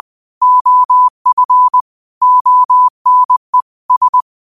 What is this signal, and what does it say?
Morse code, heard as a keyed tone, about 20 wpm, 1000 Hz. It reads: OF ONE S
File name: qotdfiles / 20wpm